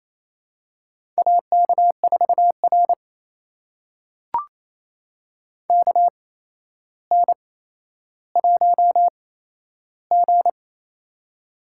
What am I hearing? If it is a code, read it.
Morse code, heard as a keyed tone, about 28 wpm, spaced as for 12 wpm, 700 Hz. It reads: AK4R KN1G
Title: USA Call Signs - 4 Characters - 8x Character Spacing - 28wpm